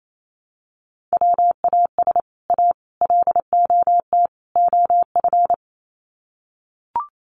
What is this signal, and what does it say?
Morse code, heard as a keyed tone, about 28 wpm, 700 Hz. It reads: WAS A LOT OF
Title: Sets of 4 Words 28wpm